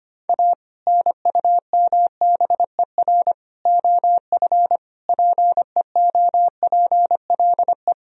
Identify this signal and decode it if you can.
Morse code, heard as a keyed tone, about 25 wpm, 700 Hz. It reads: A NUMBER OF PEOPLE
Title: Sentences from Top 1000 Words - Review Part 1 - 25wpm